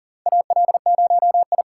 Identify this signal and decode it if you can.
Morse code, heard as a keyed tone, about 40 wpm, 700 Hz. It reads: AL0I
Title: USA Call Signs - 3x Character Spacing - 40wpm